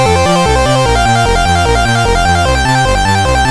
There is no code.